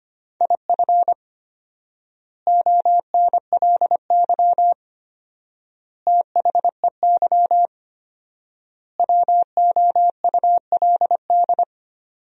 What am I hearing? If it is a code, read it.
Morse code, heard as a keyed tone, about 25 wpm, 700 Hz. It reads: IF ONLY THEY WOULD
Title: 4x Word-Spacing - Sentences from Top 500 Words - 25wpm